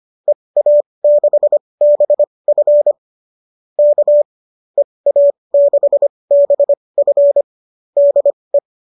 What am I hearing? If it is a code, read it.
Morse code, heard as a keyed tone, about 25 wpm, spaced as for 20 wpm, 590 Hz. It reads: EA6BF K EA6BF DE